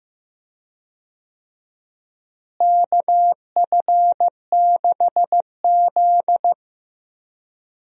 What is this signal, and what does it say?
Morse code, heard as a keyed tone, about 15 wpm, 700 Hz. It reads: KF6Z
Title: USA Call Signs - 4 Characters - Repeated 2x - 15wpm